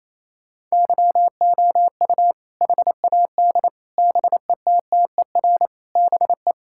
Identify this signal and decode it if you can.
Morse code, heard as a keyed tone, about 28 wpm, 700 Hz. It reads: YOU HAD BETTER BE
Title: Sentences from Top 1600 Words 28wpm